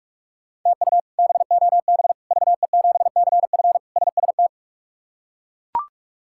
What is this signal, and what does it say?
Morse code, heard as a keyed tone, about 42 wpm, 700 Hz. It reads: TU BOB VE7CF SST
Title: K1USN SST QSO Practice - 45wpm